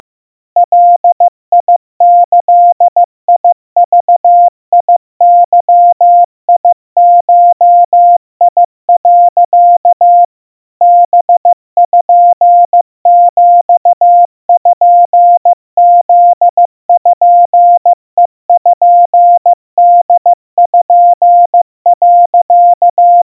Morse code (Japanese Wabun code), heard as a keyed tone, about 15 wpm, 700 Hz. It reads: ガギグゲゴ、 パピプペポ、
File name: iroha 50 cpm 700 Hz Crispy